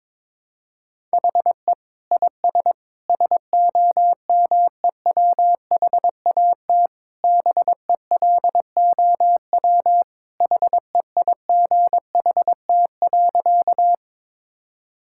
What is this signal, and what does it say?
Morse code, heard as a keyed tone, about 22 wpm, 700 Hz. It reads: HE IS SOMEWHAT BELOW HEIGHT.
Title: Sentences from Top 1600 Words 22wpm